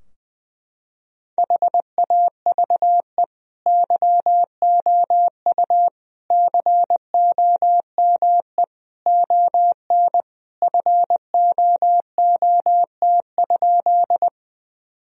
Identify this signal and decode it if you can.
Morse code, heard as a keyed tone, about 20 wpm, 700 Hz. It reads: HAVE YOU COME ON FOOT?
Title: Sentences from Top 1000 Words - Review Part 3 - 20wpm